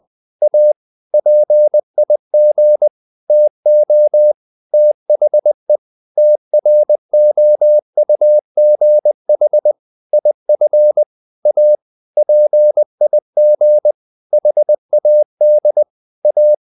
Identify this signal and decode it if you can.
Morse code, heard as a keyed tone, about 20 wpm, 600 Hz. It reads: A PIG TO THE TROUGH IF A PIG HAD A
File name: the old man and the sea part 2 20wpm